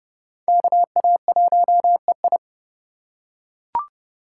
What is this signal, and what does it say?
Morse code, heard as a keyed tone, about 29 wpm, 700 Hz. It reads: KA1EI